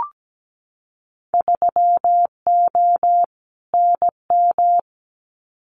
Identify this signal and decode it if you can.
Morse code, heard as a keyed tone, about 17 wpm, 700 Hz. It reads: 3O NM
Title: Winter Field Day Exchanges Encore 17wpm